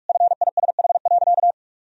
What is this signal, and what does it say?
Morse code, heard as a keyed tone, about 46 wpm, 700 Hz. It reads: FISH.
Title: Sentences from Top 1000 Words - Review Part 1 - 45wpm